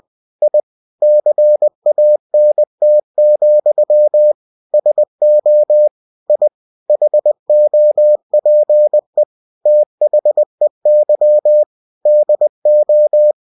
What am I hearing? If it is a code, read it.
Morse code, heard as a keyed tone, about 20 wpm, 600 Hz. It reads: I CANT, SO I HOPE THEY DO
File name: cw-books / winnie-the-pooh / winnie-20wpm-page-0093